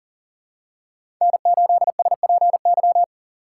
Morse code, heard as a keyed tone, about 40 wpm, 700 Hz. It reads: N8SPY
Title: USA Call Signs - 5 Characters - 6x Character Spacing - 40wpm